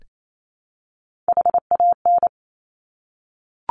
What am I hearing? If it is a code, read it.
Morse code, heard as a keyed tone, about 28 wpm, 700 Hz. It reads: HAN
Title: Top Three Letter Combinations in Words 28wpm